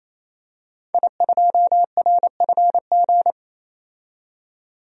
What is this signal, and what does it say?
Morse code, heard as a keyed tone, about 28 wpm, 700 Hz. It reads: I2RFG
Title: DXCC-100 Call Signs - 5 Characters - 28wpm